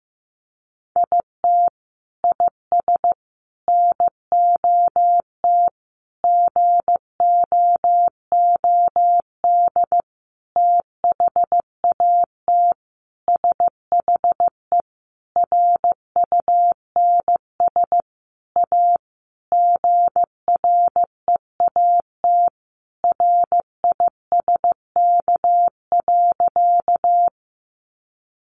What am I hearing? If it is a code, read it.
Morse code, heard as a keyed tone, about 15 wpm, 700 Hz. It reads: IT IS NOT GOOD THAT SHE RUNS A GREAT RISK.
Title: Sentences from Top 600 Words 15wpm